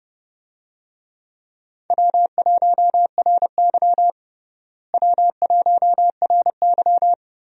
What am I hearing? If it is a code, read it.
Morse code, heard as a keyed tone, about 32 wpm, 700 Hz. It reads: W1RY W1RY
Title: USA Call Signs - 4 Characters - Repeated 3x - 30wpm